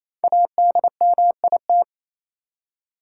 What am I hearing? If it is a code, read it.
Morse code, heard as a keyed tone, about 28 wpm, 700 Hz. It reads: ADMIT
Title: Five Letter Words - 28wpm